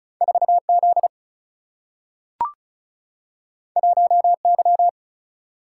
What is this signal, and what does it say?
Morse code, heard as a keyed tone, about 36 wpm, 700 Hz. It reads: 4 Z 1 Y
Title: Single-Letter Single-Number 35wpm